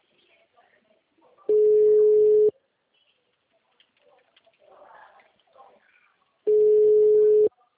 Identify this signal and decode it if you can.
Ringback tone (Europe, 425 Hz), with speech.
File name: out- -2009- - - .433